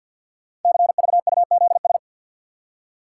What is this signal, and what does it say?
Morse code, heard as a keyed tone, about 48 wpm, 700 Hz. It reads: CVUZS
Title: Code Groups - 5 Letters - 50wpm